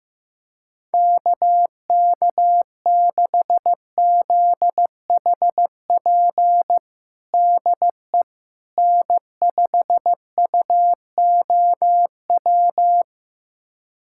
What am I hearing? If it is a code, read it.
Morse code, heard as a keyed tone, about 15 wpm, 700 Hz. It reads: KK6ZHP DE N5UOW
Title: USA Callsign DE Callsign 15wpm